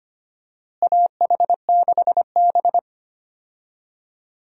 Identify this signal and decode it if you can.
Morse code, heard as a keyed tone, about 25 wpm, 700 Hz. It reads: AH6B
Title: USA Call Signs - 4 Characters - Repeated 2x - 25wpm